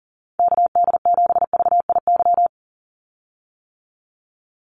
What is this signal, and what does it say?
Morse code, heard as a keyed tone, about 40 wpm, 700 Hz. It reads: KD7VIY
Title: USA Call Signs 40wpm